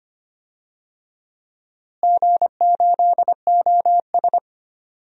Morse code, heard as a keyed tone, about 25 wpm, 700 Hz. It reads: G8OS
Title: DXCC-100 Call Signs - 4 Characters - Repeated 2x - 25wpm